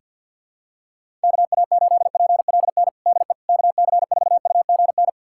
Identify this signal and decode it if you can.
Morse code, heard as a keyed tone, about 52 wpm, 700 Hz. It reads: KA8PLN DE KC4UGN
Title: USA Callsign DE Callsign 50wpm